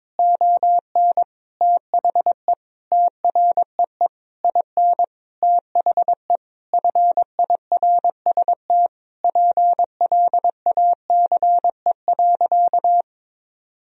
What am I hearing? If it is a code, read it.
Morse code, heard as a keyed tone, about 22 wpm, 700 Hz. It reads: ON THE TREE IN THE FIRST PLACE.
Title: Sentences from Top 1000 Words - Review Part 1 - 22wpm